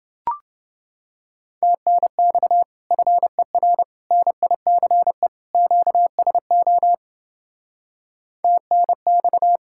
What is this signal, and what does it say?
Morse code, heard as a keyed tone, about 28 wpm, 700 Hz. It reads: TNX FER NICE QSO TNX